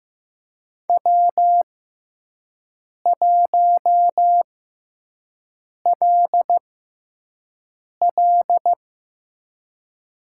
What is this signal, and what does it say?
Morse code, heard as a keyed tone, about 15 wpm, spaced as for 5 wpm, 700 Hz. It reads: W1LL
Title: USA Call Signs - 6x Character Spacing - 15wpm